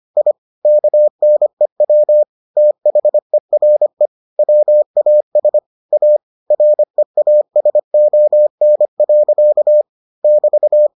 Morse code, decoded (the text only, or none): I KNEW THERE WAS A REASON. <BT>